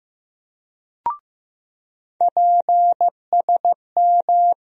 Morse code, heard as a keyed tone, about 15 wpm, 700 Hz. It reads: PSM
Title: USA Call Sign Postfixes - 3 Letters 15wpm